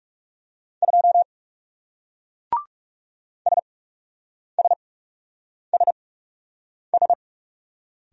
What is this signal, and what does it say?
Morse code, heard as a keyed tone, about 40 wpm, spaced as for 8 wpm, 700 Hz. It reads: 2 SSSS